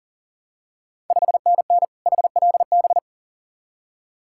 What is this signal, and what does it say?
Morse code, heard as a keyed tone, about 38 wpm, 700 Hz. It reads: 5NN HLB
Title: Ontario QSO Party Exchange 40wpm